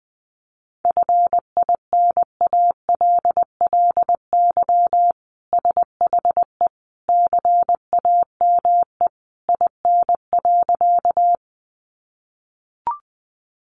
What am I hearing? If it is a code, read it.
Morse code, heard as a keyed tone, about 20 wpm, 700 Hz. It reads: FINALLY SHE CAME IN.
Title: Sentences from Top 1000 Words 20wpm